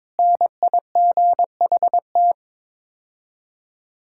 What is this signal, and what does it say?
Morse code, heard as a keyed tone, about 22 wpm, 700 Hz. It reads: NIGHT
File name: practice / 22wpm / Sets of 6 Words - 22wpm